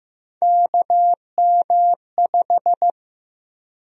Morse code, heard as a keyed tone, about 15 wpm, 700 Hz. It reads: KM5
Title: USA Call Sign Prefixes - 15wpm